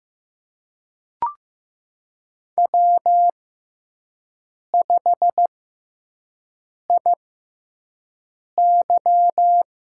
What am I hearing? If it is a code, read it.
Morse code, heard as a keyed tone, about 15 wpm, spaced as for 5 wpm, 700 Hz. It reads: W5IY
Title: USA Call Signs - 4 Characters - 6x Character Spacing - 15wpm